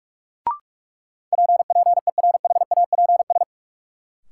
Morse code, heard as a keyed tone, about 46 wpm, 700 Hz. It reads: PPERHAPS
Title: Sentences from Top 1100 Words - New Word Drill - 45wpm